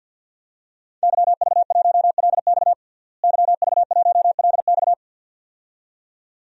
Morse code, heard as a keyed tone, about 53 wpm, 700 Hz. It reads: YV1LX YV1LX
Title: DXCC-100 Call Signs - 5 Characters - Repeated 2x - 50wpm